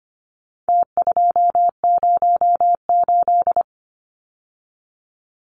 Morse code, heard as a keyed tone, about 25 wpm, 700 Hz. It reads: T208
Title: Numbers - Contest Style - 4-Digit 25wpm